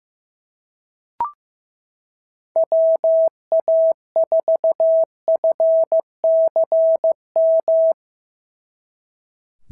Morse code, heard as a keyed tone, about 15 wpm, 650 Hz. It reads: WA4FCM
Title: SOTA QSO Practice 15wpm